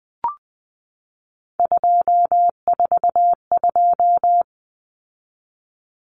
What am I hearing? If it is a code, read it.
Morse code, heard as a keyed tone, about 20 wpm, 700 Hz. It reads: 242
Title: Numbers - 3-Digit 20wpm